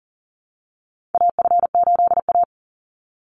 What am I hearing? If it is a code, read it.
Morse code, heard as a keyed tone, about 40 wpm, 700 Hz. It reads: AF8A
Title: Call Signs - 40wpm